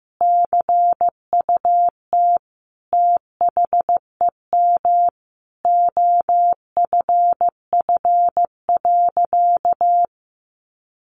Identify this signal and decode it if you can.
Morse code, heard as a keyed tone, about 15 wpm, 700 Hz. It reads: CUT THEM OFF.
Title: Sentences from Top 1000 Words 15wpm